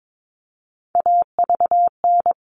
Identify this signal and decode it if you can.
Morse code, heard as a keyed tone, about 22 wpm, 700 Hz. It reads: AVN
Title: All Cut Numbers - 3-Digit 22wpm